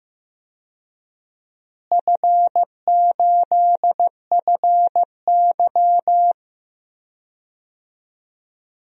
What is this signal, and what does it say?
Morse code, heard as a keyed tone, about 15 wpm, 700 Hz. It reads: F8FY
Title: DXCC-100 Call Signs - 4 Characters - Repeated 3x - 15wpm